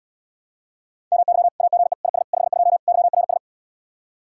Morse code, heard as a keyed tone, about 77 wpm, 700 Hz. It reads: MY AGE IS 52 YRS